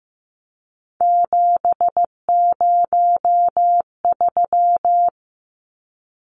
Morse code, heard as a keyed tone, about 15 wpm, 700 Hz. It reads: 703
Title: Numbers - 3-Digit 15wpm